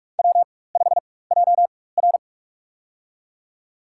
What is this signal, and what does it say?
Morse code, heard as a keyed tone, about 46 wpm, spaced as for 21 wpm, 700 Hz. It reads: W5JR
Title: USA Call Signs - 4 Characters - 4x Character Spacing - 45wpm